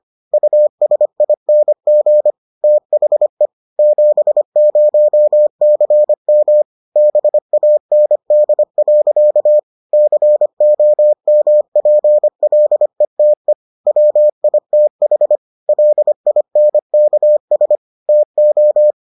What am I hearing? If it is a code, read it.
Morse code, heard as a keyed tone, about 25 wpm, 600 Hz. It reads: USING THE 70CM BAND. COMPLETE WITH LINKS TO